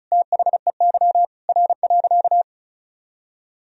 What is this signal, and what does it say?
Morse code, heard as a keyed tone, about 36 wpm, 700 Hz. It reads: THEY R.
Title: Sentences from Top 100 Words and Abbreviations 35wpm